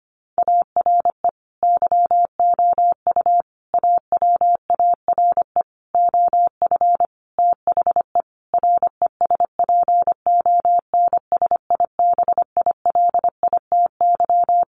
Morse code, heard as a keyed tone, about 25 wpm, 700 Hz. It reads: ARE YOU AWARE OF THE RESPONSIBILITY
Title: Sentences from Top 1500 Words 25wpm